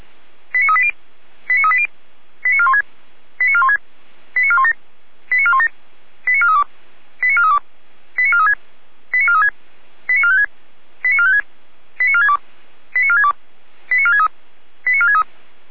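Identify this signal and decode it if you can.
Five-tone selective call (ZVEI), heard as quick runs of five tones, tones of about 70 ms. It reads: 89380 89380 89416 89416 89417 89417 89432 89432 89436 89436 89456 89456 89462 89462 89463 89463
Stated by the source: Five-tone ZVEI selective call 89403, received by SDR.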